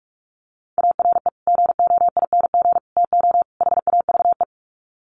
Morse code, heard as a keyed tone, about 44 wpm, 700 Hz. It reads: ARE GOING TO HAVE
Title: Sets of 4 Words 45wpm